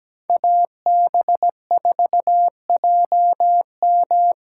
Morse code, heard as a keyed tone, about 20 wpm, 700 Hz. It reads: AB4JM
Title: Speed Racing - USA 2x2 Call Signs 17wpm